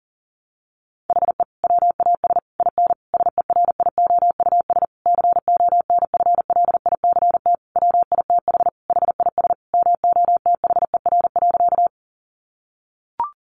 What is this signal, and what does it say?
Morse code, heard as a keyed tone, about 40 wpm, 700 Hz. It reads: HE WAS IN SERIOUS CONFLICT WITH HIS MOTHER.
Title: Sentences from Top 1300 Words 40wpm